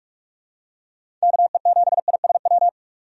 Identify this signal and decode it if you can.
Morse code, heard as a keyed tone, about 42 wpm, 700 Hz. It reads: KE7ISW
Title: USA Call Signs - 6 Characters - 45wpm